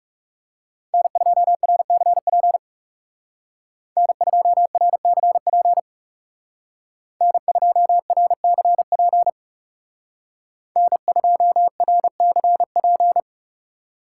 Morse code, heard as a keyed tone, about 35 wpm, 700 Hz. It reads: N2RCP N2RCP N2RCP N2RCP
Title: Speed Racing - USA 1x3 Call Signs 30wpm